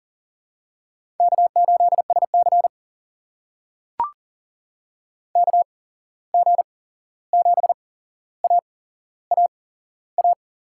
Morse code, heard as a keyed tone, about 40 wpm, spaced as for 15 wpm, 700 Hz. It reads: K8SC KG7AAA
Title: USA Call Signs - 8x Character Spacing - 40wpm